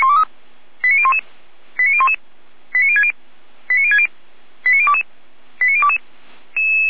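Five-tone selective call (ZVEI), heard as quick runs of five tones, tones of about 70 ms. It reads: 89011 89011 89060 89060 89022 89022
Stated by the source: ZVEI five-tone selective call received on an SDR, from the Feuerwehr Orlamünde (f85360) fire brigade audio server.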